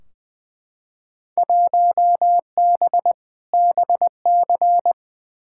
Morse code, heard as a keyed tone, about 20 wpm, 700 Hz. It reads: 1B BC
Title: Field Day Exchanges - 20wpm